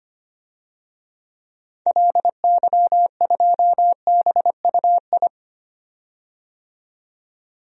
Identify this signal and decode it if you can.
Morse code, heard as a keyed tone, about 25 wpm, 700 Hz. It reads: LY2BUI
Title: DXCC-100 Call Signs - 6 Characters - 25wpm